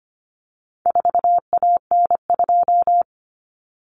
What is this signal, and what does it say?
Morse code, heard as a keyed tone, about 25 wpm, 700 Hz. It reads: 4AN2
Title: Common Cut Numbers - 4-Digit 25wpm